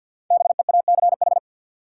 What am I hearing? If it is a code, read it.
Morse code, heard as a keyed tone, about 53 wpm, 700 Hz. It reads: BEACH
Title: Sentences from Top 1500 Words - New Word Drill Alt - 50wpm